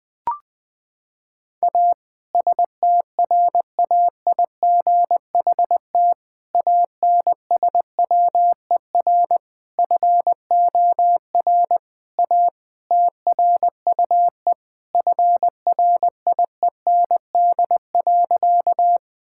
Morse code, heard as a keyed tone, about 20 wpm, 700 Hz. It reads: A STRAIGHT ANSWER FOR A TRUE FRIEND.
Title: Sentences from Top 1000 Words - Review Part 6 - 20wpm